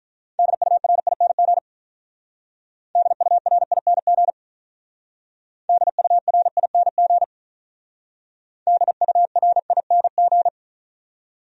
Morse code, heard as a keyed tone, about 45 wpm, 700 Hz. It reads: DURING DURING DURING DURING